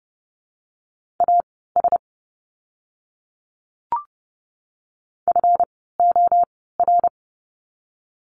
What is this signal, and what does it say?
Morse code, heard as a keyed tone, about 30 wpm, spaced as for 17 wpm, 700 Hz. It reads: AS FOR
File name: course / 30:10wpm / 237 Lesson 38f - Letter q - Words 30wpm